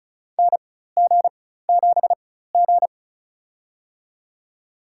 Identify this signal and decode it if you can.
Morse code, heard as a keyed tone, about 34 wpm, spaced as for 16 wpm, 700 Hz. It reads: NG7G
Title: USA Call Signs - 4 Characters - 4x Character Spacing - 35wpm